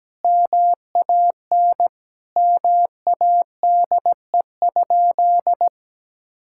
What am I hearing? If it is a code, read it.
Morse code, heard as a keyed tone, about 17 wpm, 700 Hz. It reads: MAN MADE?
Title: Sentences from Top 1000 Words - Review Part 6 - 17wpm